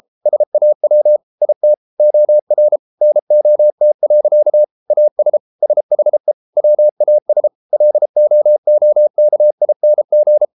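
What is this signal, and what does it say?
Morse code, heard as a keyed tone, about 33 wpm, 600 Hz. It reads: SAW IT OR NOT. AS SHE WAS LOOKING
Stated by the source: Morse code practice at 32 wpm from CW med Gnister, reading Ken Follett's novel Never in English.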